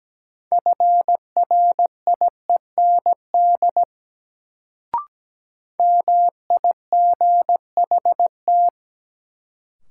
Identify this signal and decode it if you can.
Morse code, heard as a keyed tone, about 17 wpm, 700 Hz. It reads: FRIEND MIGHT
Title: Sentences from Top 300 Words - New Word Drill - 17wpm